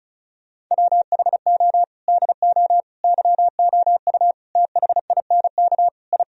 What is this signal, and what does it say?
Morse code, heard as a keyed tone, about 36 wpm, 700 Hz. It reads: WHO DO YOU THINK I
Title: Sentences from Top 1000 Words - Review Part 4 - 35wpm